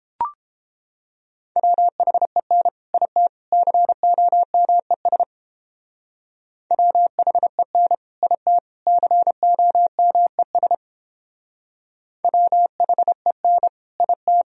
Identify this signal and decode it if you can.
Morse code, heard as a keyed tone, about 29 wpm, 700 Hz. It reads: WHEN IT COMES WHEN IT COMES WHEN IT